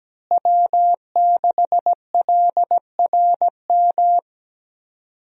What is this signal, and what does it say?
Morse code, heard as a keyed tone, about 17 wpm, 700 Hz. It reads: W6LRM
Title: USA Call Signs - 5 Characters - Repeated 3x - 17wpm